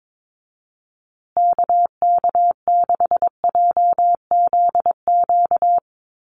Morse code, heard as a keyed tone, about 22 wpm, 700 Hz. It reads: KK6JZQ
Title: USA Call Signs Repeated 2x 22wpm